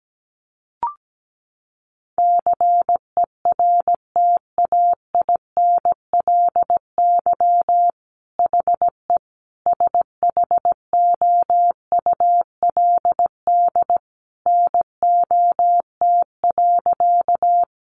Morse code, heard as a keyed tone, about 17 wpm, 700 Hz. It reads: CERTAINLY HE SHOULD NOT.
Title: Sentences from Top 900 Words 17wpm